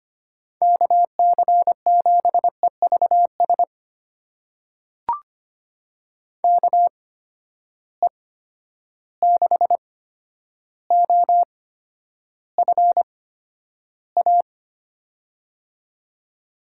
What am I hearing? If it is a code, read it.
Morse code, heard as a keyed tone, about 25 wpm, spaced as for 11 wpm, 700 Hz. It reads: KC7EVS KE6OFA
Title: USA Call Signs - 6 Characters - 8x Character Spacing - 25wpm